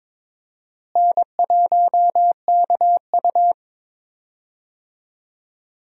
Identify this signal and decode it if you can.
Morse code, heard as a keyed tone, about 22 wpm, 700 Hz. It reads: N1KU
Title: USA Call Signs - 4 Characters - Repeated 2x - 22wpm